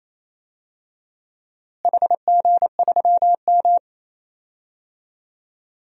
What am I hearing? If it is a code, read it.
Morse code, heard as a keyed tone, about 28 wpm, 700 Hz. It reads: HG3M
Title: DXCC-100 Call Signs - 4 Characters - Repeated 2x - 28wpm